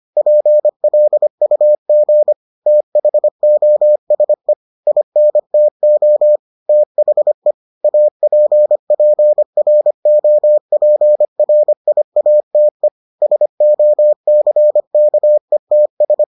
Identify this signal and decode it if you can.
Morse code, heard as a keyed tone, about 25 wpm, 600 Hz. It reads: PLUG THOSE INTO THE APPROPRIATE SOCKETS